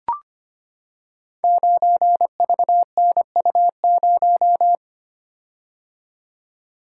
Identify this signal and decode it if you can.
Morse code, heard as a keyed tone, about 25 wpm, 700 Hz. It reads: 9VNU0
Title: Code Groups - 5 Letter:Numbers - 25wpm